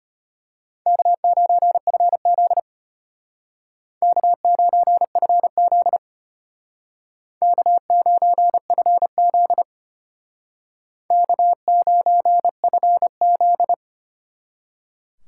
Morse code, heard as a keyed tone, about 33 wpm, 700 Hz. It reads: K9FZ K9FZ K9FZ K9FZ